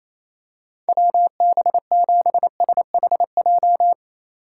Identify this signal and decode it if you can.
Morse code, heard as a keyed tone, about 28 wpm, 700 Hz. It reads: WB7SHJ
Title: USA Call Signs - 6 Characters - 4x Character Spacing - 28wpm